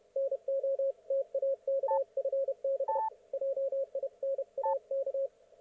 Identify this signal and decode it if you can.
Morse code, heard as a keyed tone, about 31 wpm, 530 Hz, with radio noise. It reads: NO TAK FB JINAK
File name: Archiv / CUC107 / ok1fhi cul107